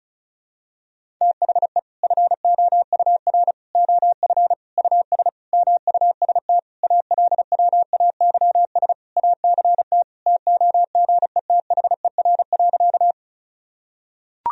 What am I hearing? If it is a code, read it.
Morse code, heard as a keyed tone, about 36 wpm, 700 Hz. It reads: THE FOUR OF US MUST ALWAYS ACT TOGETHER.
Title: Sentences from Top 1000 Words - Review Part 1 - 35wpm